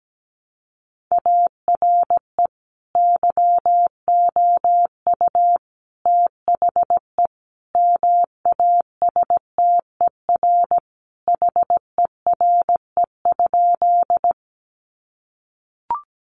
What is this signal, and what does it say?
Morse code, heard as a keyed tone, about 17 wpm, 700 Hz. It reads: ARE YOU THE MASTER HERE?
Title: Sentences from Top 1300 Words 17wpm